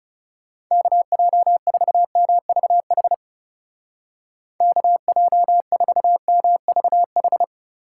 Morse code, heard as a keyed tone, about 38 wpm, 700 Hz. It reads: KJ4MVH KJ4MVH